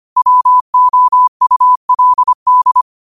Morse code, heard as a keyed tone, about 25 wpm, 1000 Hz. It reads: WOULD
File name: qotdfiles / 25wpm